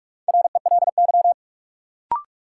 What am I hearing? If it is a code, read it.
Morse code, heard as a keyed tone, about 45 wpm, 700 Hz. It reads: RELY
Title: Sentences from Top 1200 Words - New Word Drill - 45wpm